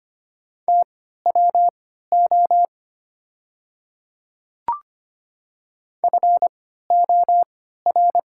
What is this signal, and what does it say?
Morse code, heard as a keyed tone, about 25 wpm, spaced as for 14 wpm, 700 Hz. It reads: TWO FOR
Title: 3x Character Spacing - Top 100 Words - 25wpm